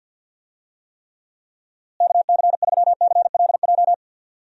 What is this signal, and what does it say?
Morse code, heard as a keyed tone, about 52 wpm, 700 Hz. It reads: KC3CLJ
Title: USA Call Signs - 6 Characters - Repeated 3x - 50wpm